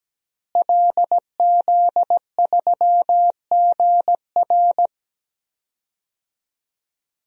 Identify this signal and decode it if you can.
Morse code, heard as a keyed tone, about 17 wpm, 700 Hz. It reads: LZ3GR
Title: DXCC-100 Call Signs - 5 Characters - Repeated 3x - 17wpm